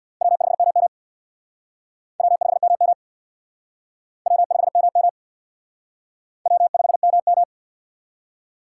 Morse code, heard as a keyed tone, about 62 wpm, 700 Hz. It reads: W5MM W5MM W5MM W5MM
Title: Speed Racing - USA 1x2 Call Signs 50wpm